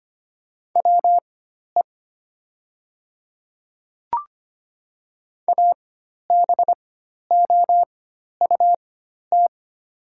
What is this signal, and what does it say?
Morse code, heard as a keyed tone, about 25 wpm, spaced as for 12 wpm, 700 Hz. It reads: WE ABOUT
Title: 4x Character Spacing - Top 100 Words - 25wpm